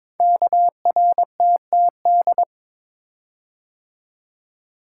Morse code, heard as a keyed tone, about 22 wpm, 700 Hz. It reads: KRTTD
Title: Code Groups - 5 Letter:Numbers - 22wpm